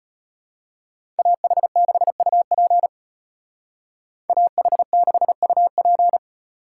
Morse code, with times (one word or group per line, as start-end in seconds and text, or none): 1.2-2.9 AH6UP
4.3-6.2 AH6UP